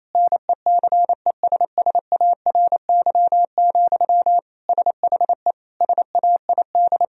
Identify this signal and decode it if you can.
Morse code, heard as a keyed tone, about 28 wpm, 700 Hz. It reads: NECESSARY, SHE SAID
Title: Sentences from Top 1000 Words - Review Part 4 - 28wpm